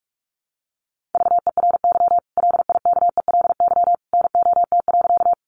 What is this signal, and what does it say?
Morse code, heard as a keyed tone, about 44 wpm, 700 Hz. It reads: VERY LIKELY NOT.
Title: Sentences from Top 600 Words 45wpm